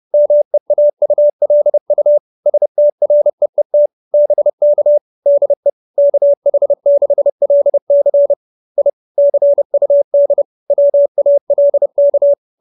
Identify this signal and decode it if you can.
Morse code, heard as a keyed tone, about 30 wpm, 570 Hz. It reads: MEAULU STREET BK DE KH6LC I CUD WALK
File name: QSO311 30